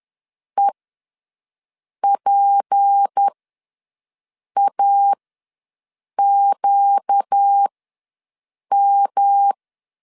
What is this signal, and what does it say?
Morse code, heard as a keyed tone, about 10 wpm, spaced as for 5 wpm, 800 Hz. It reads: EPAQM
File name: LEZIONI / 6 PKX0 / 6 P ws 10wpm 3751